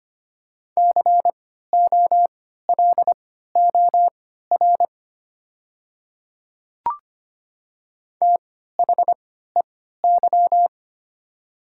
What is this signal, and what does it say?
Morse code, heard as a keyed tone, about 25 wpm, spaced as for 14 wpm, 700 Hz. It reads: COLOR THEY
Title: 3x Character Spacing - Top 300 Words - 25wpm